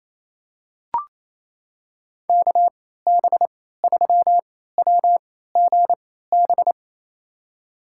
Morse code, heard as a keyed tone, about 28 wpm, spaced as for 16 wpm, 700 Hz. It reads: KB3WGB
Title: USA Call Signs - 3x Character Spacing - 28wpm